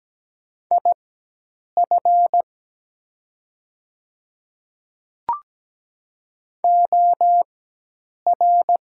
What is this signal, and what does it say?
Morse code, heard as a keyed tone, about 17 wpm, spaced as for 8 wpm, 700 Hz. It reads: IF OR
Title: 4x Character Spacing - Top 100 Words - 17wpm